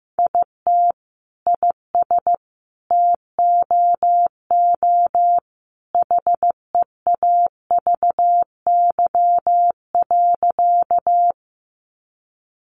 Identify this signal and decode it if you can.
Morse code, heard as a keyed tone, about 15 wpm, 700 Hz. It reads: IT IS TOO HEAVY.
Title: Sentences from Top 1100 Words 15wpm